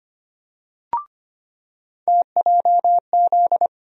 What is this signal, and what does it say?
Morse code, heard as a keyed tone, about 25 wpm, 700 Hz. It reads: TJZ
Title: USA Call Sign Postfixes - 3 Letters 25wpm